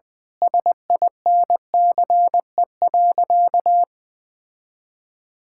Morse code, heard as a keyed tone, about 20 wpm, 700 Hz. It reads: SINCE.
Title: Sentences from Top 1600 Words 20wpm